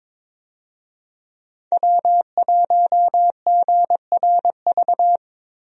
Morse code, heard as a keyed tone, about 22 wpm, 700 Hz. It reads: W1GRV
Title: USA Call Signs - 5 Characters - Repeated 2x - 22wpm